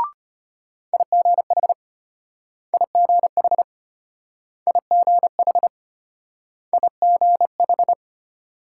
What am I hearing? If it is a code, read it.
Morse code, heard as a keyed tone, about 34 wpm, 700 Hz. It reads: IGH IGH IGH IGH